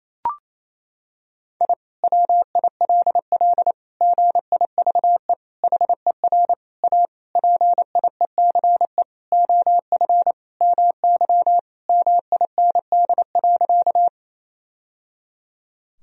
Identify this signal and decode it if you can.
Morse code, heard as a keyed tone, about 28 wpm, 700 Hz. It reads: I WILL GIVE HER A PIECE OF MY MIND.
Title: Sentences from Top 1000 Words - Review Part 2 - 28wpm